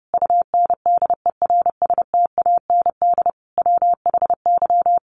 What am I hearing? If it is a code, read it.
Morse code, heard as a keyed tone, about 30 wpm, 700 Hz. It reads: UNDERSTAND WHY
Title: Sets of 5 Words Encore 30wpm